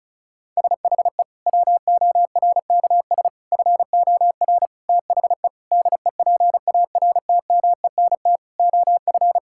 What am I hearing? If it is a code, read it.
Morse code, heard as a keyed tone, about 34 wpm, 700 Hz. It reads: SHE WORKS FOR THE DEPARTMENT OF
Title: Sentences from Top 1000 Words - Review Part 5 - 35wpm